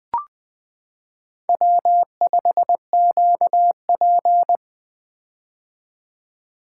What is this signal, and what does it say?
Morse code, heard as a keyed tone, about 20 wpm, 700 Hz. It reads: W5QP
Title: K1USN SST QSO Practice - 20wpm